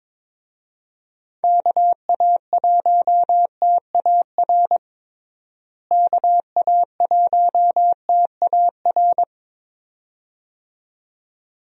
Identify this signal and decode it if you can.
Morse code, heard as a keyed tone, about 22 wpm, 700 Hz. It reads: KA1TAR KA1TAR
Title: USA Call Signs - 6 Characters - Repeated 2x - 22wpm